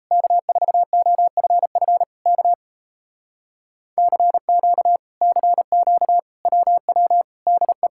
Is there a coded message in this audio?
Yes: Morse code from 0.1 to 7.9 s, heard as a keyed tone, about 38 wpm, 700 Hz. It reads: K4OFF K CQ CQ WW DE